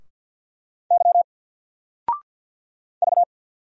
Morse code, heard as a keyed tone, about 48 wpm, 700 Hz. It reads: Y V
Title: Single Character (Letter:Number:Punctuation:Prosign) Alt 50wpm